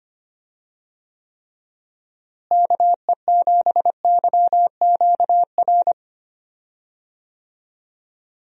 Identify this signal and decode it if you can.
Morse code, heard as a keyed tone, about 25 wpm, 700 Hz. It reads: KE7YQR
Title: USA Call Signs - 6 Characters - Repeated 2x - 25wpm